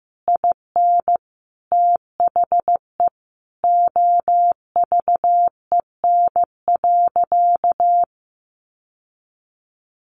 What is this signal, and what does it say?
Morse code, heard as a keyed tone, about 15 wpm, 700 Hz. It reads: IN THE OVEN.
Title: Sentences from Top 1200 Words 15wpm